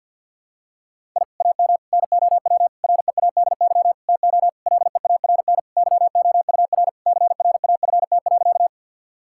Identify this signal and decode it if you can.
Morse code, heard as a keyed tone, about 50 wpm, 700 Hz. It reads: I AM NOW READY TO LEARN YOUR CRAFT.
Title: Sentences from Top 1000 Words - Review Part 6 - 50wpm